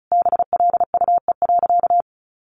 Morse code, heard as a keyed tone, about 34 wpm, 700 Hz. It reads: BLUE.
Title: A Sound Of Thunder 35wpm